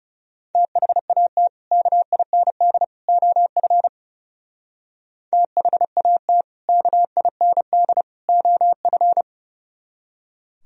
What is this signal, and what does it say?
Morse code, heard as a keyed tone, about 36 wpm, 700 Hz. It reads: THAT KIND OF THAT KIND OF
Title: Speed-Racing - Sets of 3 Words 30wpm